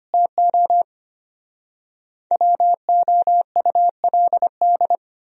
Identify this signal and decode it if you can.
Morse code, heard as a keyed tone, about 33 wpm, 700 Hz. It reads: TO WOULD